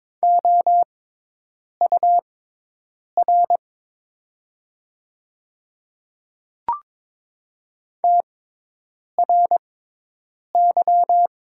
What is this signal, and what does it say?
Morse code, heard as a keyed tone, about 22 wpm, spaced as for 8 wpm, 700 Hz. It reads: OUR TRY
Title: 6x Character Spacing - Top 300 Words - 22wpm